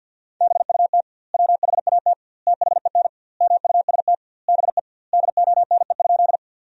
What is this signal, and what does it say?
Morse code, heard as a keyed tone, about 51 wpm, 700 Hz. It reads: BUT WHAT THEN MUST BE DONE?
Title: 2x Word-Spacing - Sentences from Top 500 Words - 50wpm